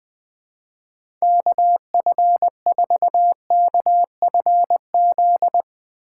Morse code, heard as a keyed tone, about 20 wpm, 700 Hz. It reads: KF4KFZ